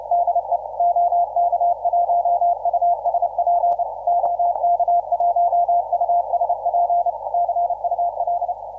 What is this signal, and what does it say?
Morse code, heard as a keyed tone, about 30 wpm, 700 Hz. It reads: DE OK1USP OK1USP CUL